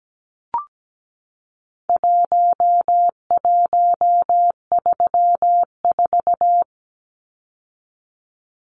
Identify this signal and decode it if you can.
Morse code, heard as a keyed tone, about 17 wpm, 700 Hz. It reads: 1134
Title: Numbers - Contest Style - 4-Digit 17wpm